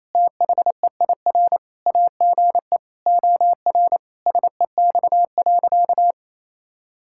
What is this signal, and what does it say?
Morse code, heard as a keyed tone, about 28 wpm, 700 Hz. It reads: THEIR AGE OR SEX.
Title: Sentences from Top 1600 Words 28wpm